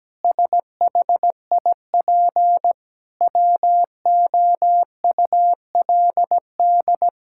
Morse code, heard as a keyed tone, about 17 wpm, 700 Hz. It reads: SHIP WOULD
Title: Sentences from Top 1000 Words - Review Part 6 - 17wpm